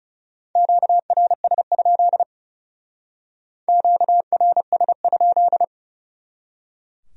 Morse code, heard as a keyed tone, about 38 wpm, 700 Hz. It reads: QRS? QRS?